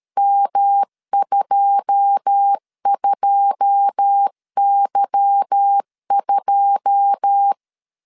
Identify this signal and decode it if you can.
Morse code, heard as a keyed tone, about 13 wpm, 800 Hz. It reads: M22Y2
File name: LEZIONI / 9 2 12wpm 2301